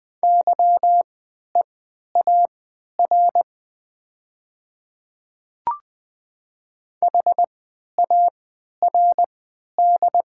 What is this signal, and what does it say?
Morse code, heard as a keyed tone, about 20 wpm, spaced as for 11 wpm, 700 Hz. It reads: YEAR HARD
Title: 3x Character Spacing - Top 300 Words - 20wpm